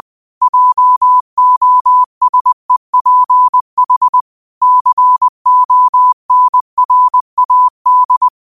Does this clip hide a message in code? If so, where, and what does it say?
Yes: Morse code from 0.4 to 8.3 s, heard as a keyed tone, about 20 wpm, 1000 Hz. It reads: JOSEPH CONRAD